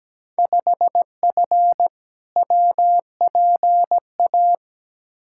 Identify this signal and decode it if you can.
Morse code, heard as a keyed tone, about 17 wpm, 700 Hz. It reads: 5F WPA
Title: Field Day Exchanges Encore - 17wpm